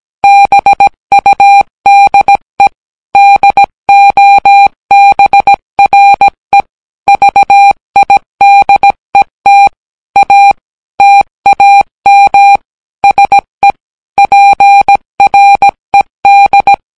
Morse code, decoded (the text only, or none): BUDE DOBRE VIDET A TAM SE PRED